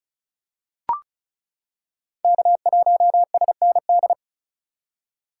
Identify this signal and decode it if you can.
Morse code, heard as a keyed tone, about 36 wpm, 700 Hz. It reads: K1SND
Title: K1USN SST QSO Practice - 35wpm